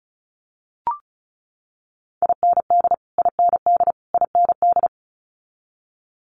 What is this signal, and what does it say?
Morse code, heard as a keyed tone, about 36 wpm, 700 Hz. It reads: IND IND IND
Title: Multi-letter Phonemes Repeated 3x 35wpm